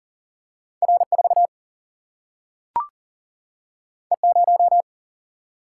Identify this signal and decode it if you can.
Morse code, heard as a keyed tone, about 39 wpm, 700 Hz. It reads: R 4 E 0
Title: Single-Letter Single-Number 40wpm